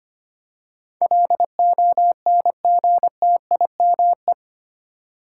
Morse code, heard as a keyed tone, about 25 wpm, 700 Hz. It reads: LONGTIME
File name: Eight Letter Words - 25wpm